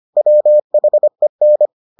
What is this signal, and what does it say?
Morse code, heard as a keyed tone, about 25 wpm, 600 Hz. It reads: WHEN